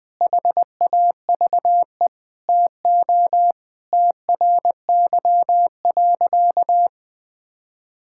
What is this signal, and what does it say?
Morse code, heard as a keyed tone, about 21 wpm, 700 Hz. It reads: HAVE TO TRY.
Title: Sentences with 5-Word Phrases - 20wpm